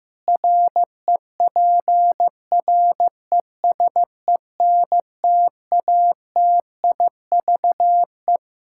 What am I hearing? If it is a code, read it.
Morse code, heard as a keyed tone, about 15 wpm, 700 Hz. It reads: REPRESENTATIVE
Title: Practice QSO - Exchange 3 - More Info - 15wpm